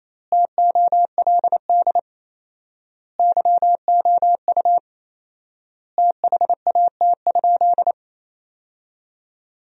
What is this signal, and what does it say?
Morse code, heard as a keyed tone, about 28 wpm, 700 Hz. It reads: TOLD YOU THAT?
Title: 4x Word-Spacing - Sentences from Top 400 Words - 28wpm